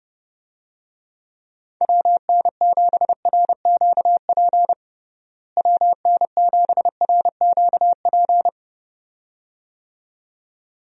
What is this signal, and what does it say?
Morse code, heard as a keyed tone, about 32 wpm, 700 Hz. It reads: WN7RQP WN7RQP
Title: USA Call Signs - 6 Characters - Repeated 2x - 30wpm